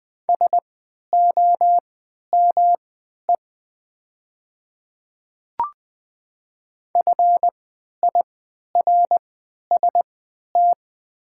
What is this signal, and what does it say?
Morse code, heard as a keyed tone, about 20 wpm, spaced as for 11 wpm, 700 Hz. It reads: SOME FIRST
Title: 3x Character Spacing - Top 100 Words - 20wpm